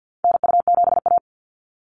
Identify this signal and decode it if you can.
Morse code, heard as a keyed tone, about 50 wpm, 700 Hz. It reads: NU7A
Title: NA SOTA Activator Callsigns 50wpm